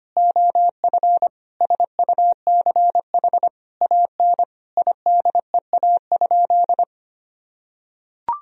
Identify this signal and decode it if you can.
Morse code, heard as a keyed tone, about 25 wpm, 700 Hz. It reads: OF SUCH AN IDEA?
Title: Sentences from Top 400 Words and Abbreviations 25wpm